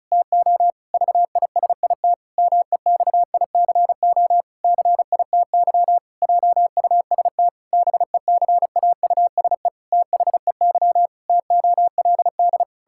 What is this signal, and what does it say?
Morse code, heard as a keyed tone, about 34 wpm, 700 Hz. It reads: TO VISIT MEXICO CITY JUST BECAUSE THEY TOLD
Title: Pangram Challenge v2 - 35wpm